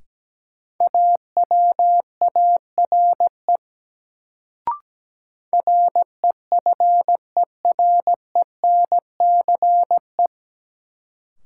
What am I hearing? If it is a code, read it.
Morse code, heard as a keyed tone, about 17 wpm, 700 Hz. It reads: AWARE REFERENCE
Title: Sentences from Top 1200 Words - New Word Drill - 17wpm